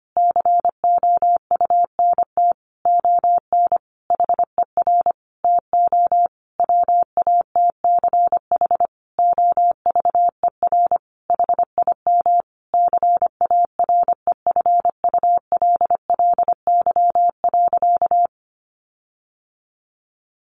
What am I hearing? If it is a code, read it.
Morse code, heard as a keyed tone, about 25 wpm, 700 Hz. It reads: COUNT ON HER TO WATCH OVER HIM CAREFULLY.
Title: Sentences from Top 1500 Words 25wpm